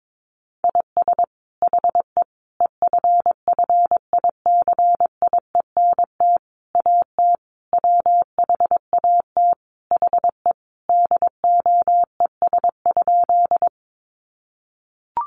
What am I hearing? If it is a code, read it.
Morse code, heard as a keyed tone, about 22 wpm, 700 Hz. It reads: IS HE EFFICIENT AT WHAT HE DOES?
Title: Sentences from Top 1300 Words 22wpm